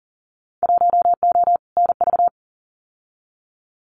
Morse code, heard as a keyed tone, about 39 wpm, 700 Hz. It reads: 1O NV
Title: Winter Field Day Exchanges Encore 40wpm